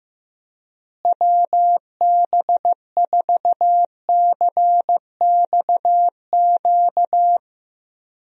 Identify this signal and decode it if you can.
Morse code, heard as a keyed tone, about 15 wpm, 700 Hz. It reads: WB4CXQ
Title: USA Call Signs - 6 Characters - Repeated 2x - 15wpm